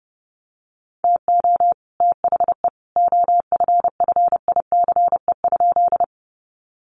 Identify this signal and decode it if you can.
Morse code, heard as a keyed tone, about 30 wpm, 700 Hz. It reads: TO THE OFFICE?
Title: Sentences from Top 1000 Words 30wpm